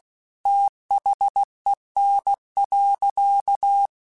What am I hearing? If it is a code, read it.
Morse code, heard as a keyed tone, about 16 wpm, 800 Hz. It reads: THEN.